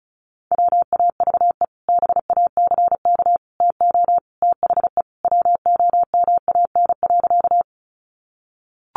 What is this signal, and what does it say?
Morse code, heard as a keyed tone, about 34 wpm, 700 Hz. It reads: WAVE BACK TO THE WOMAN.
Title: Sentences from Top 1400 Words 35wpm